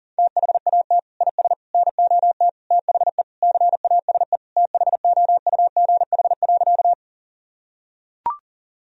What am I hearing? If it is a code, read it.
Morse code, heard as a keyed tone, about 40 wpm, 700 Hz. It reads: THAT IS NOT THE CASE THOUGH.
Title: Sentences from Top 1000 Words - Review Part 2 - 40wpm